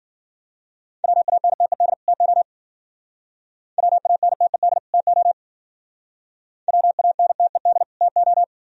Morse code, heard as a keyed tone, about 52 wpm, 700 Hz. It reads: WANTED TO WANTED TO WANTED TO